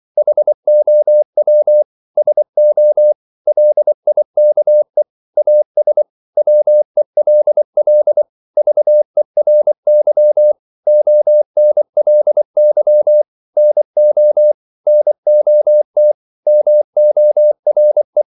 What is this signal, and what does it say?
Morse code, heard as a keyed tone, about 24 wpm, 600 Hz. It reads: HOW SO LIKE AS WELL VERY ONLY NO NOT MORE
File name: CW / Practice / 100MostCommonEnglishWords24